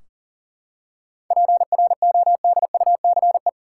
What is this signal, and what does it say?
Morse code, heard as a keyed tone, about 40 wpm, 700 Hz. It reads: PRODUCE
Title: Seven Letter Words - 40wpm